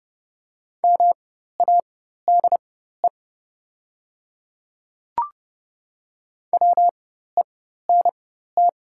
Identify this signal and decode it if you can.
Morse code, heard as a keyed tone, about 30 wpm, spaced as for 14 wpm, 700 Hz. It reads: MADE WENT
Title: 4x Character Spacing - Top 300 Words - 30wpm